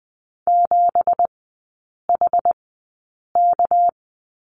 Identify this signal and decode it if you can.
Morse code, heard as a keyed tone, about 20 wpm, spaced as for 8 wpm, 700 Hz. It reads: 7HK
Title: 198 Lesson 32e - Letter k - Three Characters 20wpm